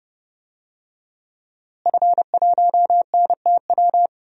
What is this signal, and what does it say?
Morse code, heard as a keyed tone, about 30 wpm, 700 Hz. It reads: F1NTW
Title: DXCC-100 Call Signs - 5 Characters - Repeated 2x - 30wpm